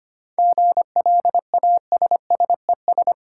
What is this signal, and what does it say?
Morse code, heard as a keyed tone, about 25 wpm, 690 Hz. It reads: GLASSES
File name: practice / 25wpm / Easy Words with Hints - 25wpm